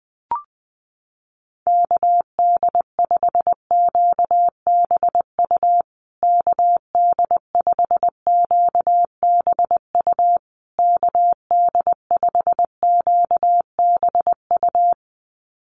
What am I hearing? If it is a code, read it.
Morse code, heard as a keyed tone, about 20 wpm, 700 Hz. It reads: KD5QBU KD5QBU KD5QBU
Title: USA Call Signs Repeated 3x 20wpm